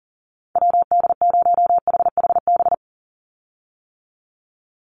Morse code, heard as a keyed tone, about 40 wpm, 700 Hz. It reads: WD0HHB
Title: USA Call Signs 40wpm